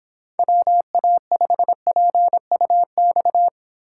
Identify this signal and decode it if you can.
Morse code, heard as a keyed tone, about 28 wpm, 700 Hz. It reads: WA5PUX